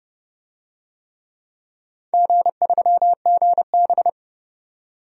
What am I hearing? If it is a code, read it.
Morse code, heard as a keyed tone, about 30 wpm, 700 Hz. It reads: G3GB
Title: DXCC-100 Call Signs - 4 Characters - 30wpm